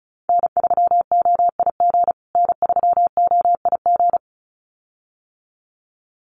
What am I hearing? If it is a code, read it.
Morse code, heard as a keyed tone, about 34 wpm, 700 Hz. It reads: N3OIG N3OIG
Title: USA Call Signs Repeated 2x 35wpm